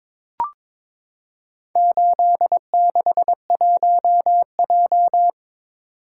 Morse code, heard as a keyed tone, about 22 wpm, 700 Hz. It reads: 861J
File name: practice / 22wpm / Four Letter-Number Combinations - 22wpm